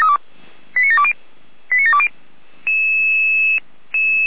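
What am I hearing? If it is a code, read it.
Five-tone selective call (ZVEI), heard as quick runs of five tones, tones of about 70 ms. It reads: 89720 89720
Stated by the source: Five-tone ZVEI selective call received by SDR.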